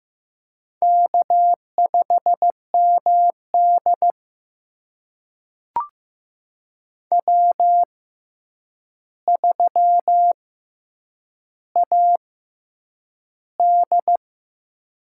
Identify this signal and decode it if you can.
Morse code, heard as a keyed tone, about 15 wpm, spaced as for 8 wpm, 700 Hz. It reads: K5MD W3AD
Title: USA Call Signs - 4 Characters - 6x Character Spacing - 15wpm